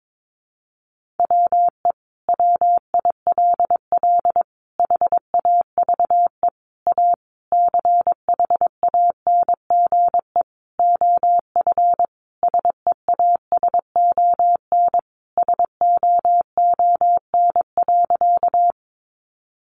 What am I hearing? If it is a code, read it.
Morse code, heard as a keyed tone, about 22 wpm, 700 Hz. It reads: WE WILL HAVE A CHANGE OF SEASON SOON.
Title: Sentences from Top 1400 Words 22wpm